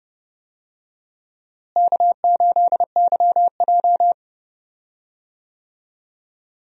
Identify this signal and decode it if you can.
Morse code, heard as a keyed tone, about 32 wpm, 700 Hz. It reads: K8YJ
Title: USA Call Signs - 4 Characters - Repeated 3x - 30wpm